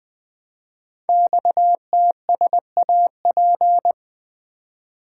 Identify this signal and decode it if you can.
Morse code, heard as a keyed tone, about 20 wpm, 700 Hz. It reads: XTSAP
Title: Code Groups - 5 Letters - 20wpm